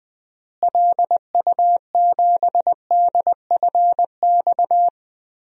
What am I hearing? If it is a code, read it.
Morse code, heard as a keyed tone, about 20 wpm, 690 Hz. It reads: LU7DFX
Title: DXCC-100 Call Signs - 6 Characters - Repeated 2x - 20wpm